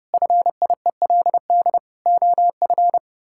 Morse code, heard as a keyed tone, about 30 wpm, 700 Hz. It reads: FIELD OF
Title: Pangram Challenge v2 - 30wpm